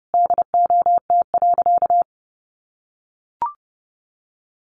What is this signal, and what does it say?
Morse code, heard as a keyed tone, about 30 wpm, 700 Hz. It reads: DOT.
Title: Sentences from Top 1500 Words 30wpm